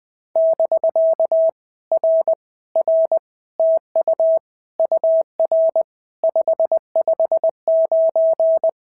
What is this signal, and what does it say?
Morse code, heard as a keyed tone, about 20 wpm, 650 Hz. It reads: <BK> R R TU UR 559
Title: POTA QSO Practice - 20wpm